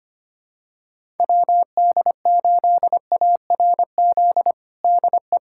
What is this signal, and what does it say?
Morse code, heard as a keyed tone, about 25 wpm, 700 Hz. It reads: WD8ARZ DE
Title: USA Callsign DE Callsign - 6 Characters - 25wpm